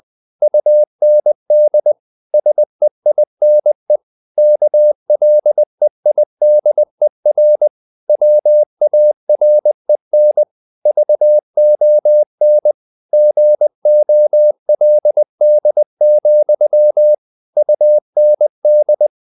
Morse code, heard as a keyed tone, about 20 wpm, 600 Hz. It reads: UND SEINE KLEIDER WAREN VON GOLD, UND